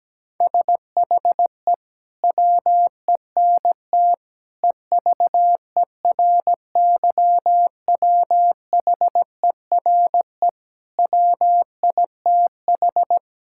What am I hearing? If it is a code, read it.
Morse code, heard as a keyed tone, about 17 wpm, 700 Hz. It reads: SHE WENT EVERYWHERE WITH
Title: Sentences from Top 1600 Words 17wpm